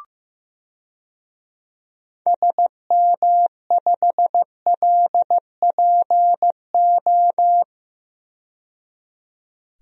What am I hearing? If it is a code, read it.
Morse code, heard as a keyed tone, about 15 wpm, 700 Hz. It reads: SM5LPO
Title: DXCC-100 Call Signs - 6 Characters - 15wpm